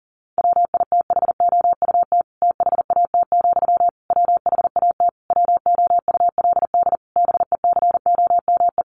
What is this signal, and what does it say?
Morse code, heard as a keyed tone, about 37 wpm, 700 Hz. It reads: WITHOUT THAT, WHAT WOULD BECOME
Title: Sentences from Top 600 Words 40wpm